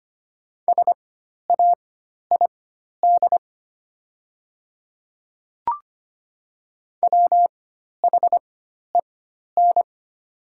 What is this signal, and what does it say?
Morse code, heard as a keyed tone, about 25 wpm, spaced as for 12 wpm, 700 Hz. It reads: SAID WHEN
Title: 4x Character Spacing - Top 300 Words - 25wpm